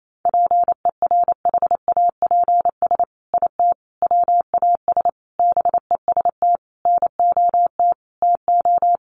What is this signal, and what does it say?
Morse code, heard as a keyed tone, about 28 wpm, 700 Hz. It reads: PERHAPS IT WAS BEST NOT TO
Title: Sentences from Top 600 Words 28wpm